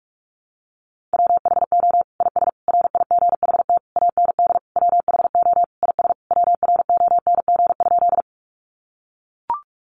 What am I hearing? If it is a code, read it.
Morse code, heard as a keyed tone, about 44 wpm, 700 Hz. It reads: WHO IS RIGHT AND WHO IS WRONG?
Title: Sentences from Top 900 Words 45wpm